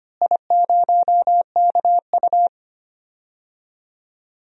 Morse code, heard as a keyed tone, about 25 wpm, 700 Hz. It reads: I0KU
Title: DXCC-100 Call Signs - 4 Characters - Repeated 3x - 25wpm